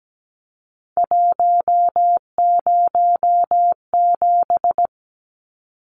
Morse code, heard as a keyed tone, about 17 wpm, 700 Hz. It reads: 107
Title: Numbers - 3-Digit 17wpm